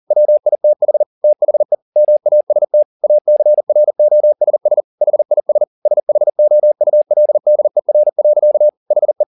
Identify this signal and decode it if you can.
Morse code, heard as a keyed tone, about 40 wpm, 600 Hz. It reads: WITH THE MAST ACROSS HIS SHOULDER. HE
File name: the old man and the sea 2 40wpm